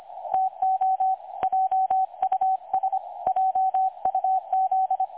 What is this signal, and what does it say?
Morse code, heard as a keyed tone, about 25 wpm, 760 Hz, with radio noise. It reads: TO JUS JUZ
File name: dlazaawansowanych / nagraniazpasma / websdr recording 3565.0kHz